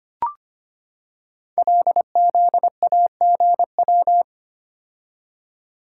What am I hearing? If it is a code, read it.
Morse code, heard as a keyed tone, about 25 wpm, 700 Hz. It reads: LZAGW
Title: Code Groups - 5 Letter:Numbers - 25wpm